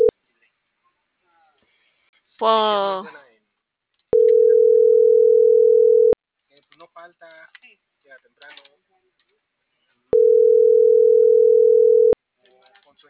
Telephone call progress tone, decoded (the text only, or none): ringback tone